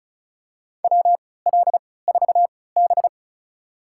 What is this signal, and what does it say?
Morse code, heard as a keyed tone, about 36 wpm, spaced as for 20 wpm, 700 Hz. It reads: WL4B
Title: USA Call Signs - 4 Characters - 3x Character Spacing - 35wpm